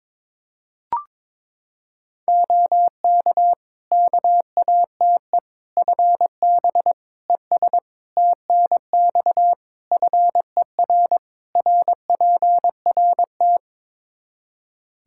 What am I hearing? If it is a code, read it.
Morse code, heard as a keyed tone, about 22 wpm, 700 Hz. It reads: OK KATE FB ES TNX FER RPRT
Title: Practice QSO - Exchange 2 - Info - 22wpm